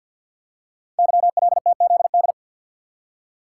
Morse code, heard as a keyed tone, about 48 wpm, 700 Hz. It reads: YLTZD
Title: Code Groups - 5 Letter:Numbers - 50wpm